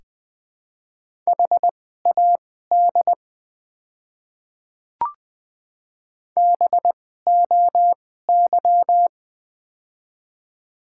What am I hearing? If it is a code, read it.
Morse code, heard as a keyed tone, about 20 wpm, spaced as for 14 wpm, 700 Hz. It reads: HAD BOY
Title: 2x Character Spacing - Top 500 Words - 20wpm